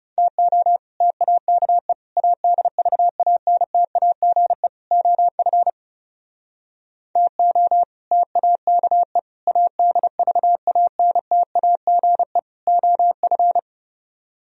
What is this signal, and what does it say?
Morse code, heard as a keyed tone, about 37 wpm, 700 Hz. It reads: TO TAKE ADVANTAGE OF TO TAKE ADVANTAGE OF
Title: Speed-Racing - Sets of 4 Words 30wpm